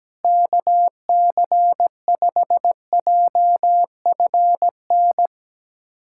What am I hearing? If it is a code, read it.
Morse code, heard as a keyed tone, about 17 wpm, 700 Hz. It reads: KC5JFN
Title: USA Callsign DE Callsign - 6 Characters - 17wpm